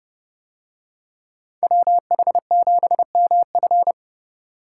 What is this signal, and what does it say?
Morse code, heard as a keyed tone, about 28 wpm, 700 Hz. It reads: WH7MF